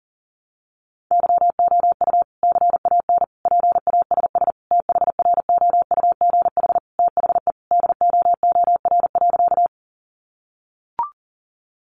Morse code, heard as a keyed tone, about 40 wpm, 700 Hz. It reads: YOU CAN PASS THROUGH THE DOOR.
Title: Sentences from Top 400 Words 40wpm